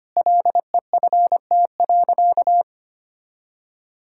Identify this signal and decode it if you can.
Morse code, heard as a keyed tone, about 25 wpm, 700 Hz. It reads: LEFT.
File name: practice / 25wpm / Sentences with 3-Word Phrases - 25wpm